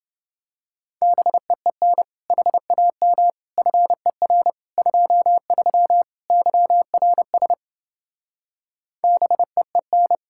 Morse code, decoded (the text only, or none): BEEN HAM FER 23 YRS BEEN